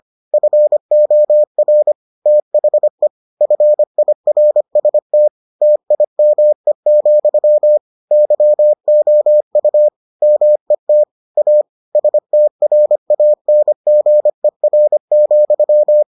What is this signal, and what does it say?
Morse code, heard as a keyed tone, about 25 wpm, 600 Hz. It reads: FOR THE FIRST TIME, YOU MET A STRANGER,